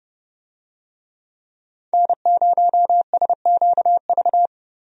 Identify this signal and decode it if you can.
Morse code, heard as a keyed tone, about 30 wpm, 700 Hz. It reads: N0SQV
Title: USA Call Signs - 5 Characters - 30wpm